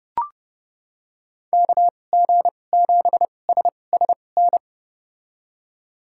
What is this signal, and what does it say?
Morse code, heard as a keyed tone, about 30 wpm, spaced as for 22 wpm, 700 Hz. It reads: KG7SSN
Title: USA Call Signs - 6 Characters - 2x Character Spacing - 30wpm